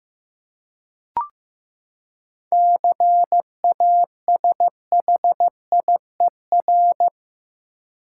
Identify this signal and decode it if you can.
Morse code, heard as a keyed tone, about 15 wpm, 700 Hz. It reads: CASHIER
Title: Practice QSO - Exchange 3 - More Info - 15wpm